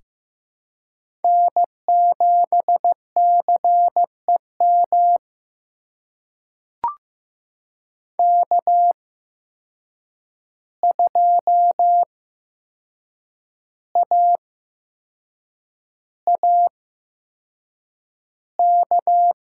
Morse code, heard as a keyed tone, about 15 wpm, spaced as for 6 wpm, 700 Hz. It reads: N7CEM K2AAK
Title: USA Call Signs - 5 Characters - 8x Character Spacing - 15wpm